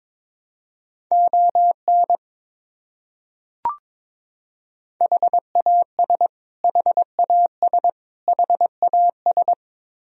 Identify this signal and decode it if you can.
Morse code, heard as a keyed tone, about 22 wpm, 700 Hz. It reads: ON HAS HAS HAS
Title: Top 100 Words Repeated 22wpm